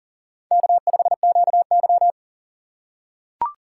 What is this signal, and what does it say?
Morse code, heard as a keyed tone, about 40 wpm, 700 Hz. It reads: K5QY
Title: USA Call Signs - 4 Characters - 3x Character Spacing - 40wpm